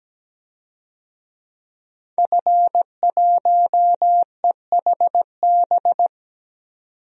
Morse code, heard as a keyed tone, about 17 wpm, 700 Hz. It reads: F1EHB
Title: DXCC-100 Call Signs - 5 Characters - 17wpm